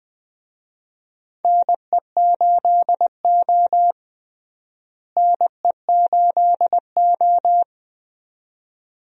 Morse code, heard as a keyed tone, about 20 wpm, 700 Hz. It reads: NE8O NE8O
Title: USA Call Signs - 4 Characters - Repeated 2x - 20wpm